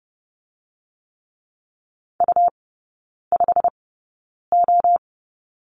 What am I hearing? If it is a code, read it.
Morse code, heard as a keyed tone, about 30 wpm, spaced as for 9 wpm, 690 Hz. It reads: U5O
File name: Lesson 17e - Letter u - Three Characters 30wpm